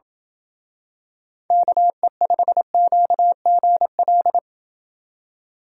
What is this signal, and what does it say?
Morse code, heard as a keyed tone, about 28 wpm, 700 Hz. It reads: KE5QGL